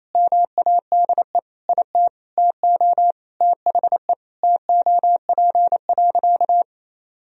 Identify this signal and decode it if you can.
Morse code, heard as a keyed tone, about 28 wpm, 700 Hz. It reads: MADE IT TO THE TOP.
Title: Sentences from Top 1000 Words - Review Part 2 - 28wpm